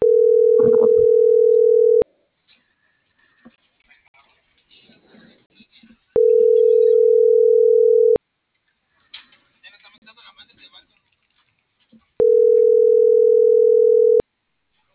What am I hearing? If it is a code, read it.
Ringback tone (North America), 440 + 480 Hz.